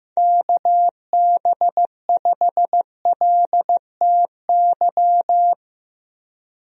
Morse code, heard as a keyed tone, about 15 wpm, 700 Hz. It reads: KB5LTY
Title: USA Call Signs - 6 Characters - 15wpm